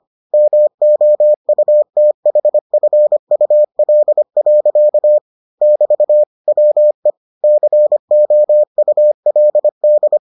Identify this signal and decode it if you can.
Morse code, heard as a keyed tone, about 25 wpm, 600 Hz. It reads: MOUTHFUL. <BT> WE COULD